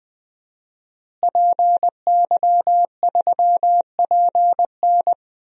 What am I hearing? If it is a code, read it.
Morse code, heard as a keyed tone, about 20 wpm, 700 Hz. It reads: PY3PN
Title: DXCC-100 Call Signs - 5 Characters - Repeated 2x - 20wpm